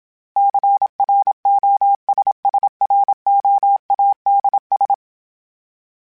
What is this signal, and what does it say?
Morse code, heard as a keyed tone, about 27 wpm, 800 Hz. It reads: CROSSROADS
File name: fromSMs / FromSMs mp3s / Cities@25